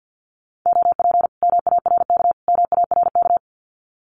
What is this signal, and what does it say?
Morse code, heard as a keyed tone, about 50 wpm, 700 Hz. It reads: OP MARK MARK
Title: Common QSO Phrases Encore 50wpm